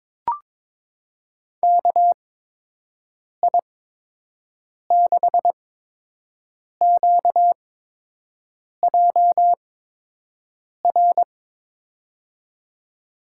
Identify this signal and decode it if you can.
Morse code, heard as a keyed tone, about 23 wpm, spaced as for 6 wpm, 700 Hz. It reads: KI6QJR
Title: USA Call Signs - 8x Character Spacing - 22wpm